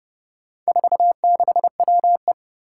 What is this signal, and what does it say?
Morse code, heard as a keyed tone, about 30 wpm, 700 Hz. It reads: 46WE
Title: Four Challenging Letter-Number Combinations 30wpm